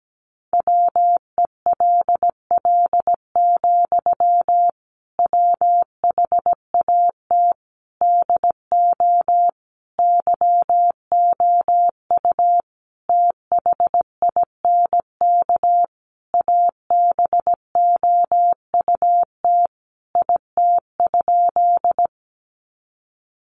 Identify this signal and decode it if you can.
Morse code, heard as a keyed tone, about 17 wpm, 700 Hz. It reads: WELL, WHAT DO YOU THINK ABOUT IT?
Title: Sentences from Top 200 Words 17wpm